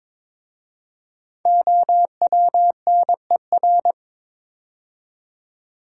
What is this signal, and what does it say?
Morse code, heard as a keyed tone, about 22 wpm, 700 Hz. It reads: OWNER
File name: 5 Letter Words - First 3 Letters Given - 22wpm